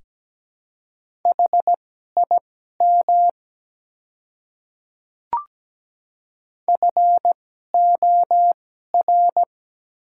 Morse code, heard as a keyed tone, about 17 wpm, spaced as for 12 wpm, 700 Hz. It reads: HIM FOR
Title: 2x Character Spacing - Top 100 Words - 17wpm